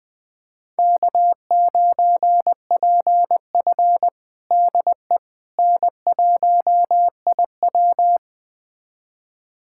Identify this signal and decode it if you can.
Morse code, heard as a keyed tone, about 20 wpm, 700 Hz. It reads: K9PF DE N1IW
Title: USA Callsign DE Callsign - 4 Characters - 20wpm